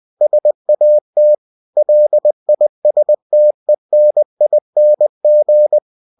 Morse code, heard as a keyed tone, about 20 wpm, 600 Hz. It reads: SAT LISTENING
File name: cw-books / winnie-the-pooh / winnie-20wpm-page-0063